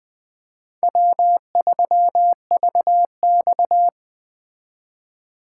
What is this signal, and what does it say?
Morse code, heard as a keyed tone, about 20 wpm, 700 Hz. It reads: W3VX
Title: Speed Racing - USA Call Signs 17wpm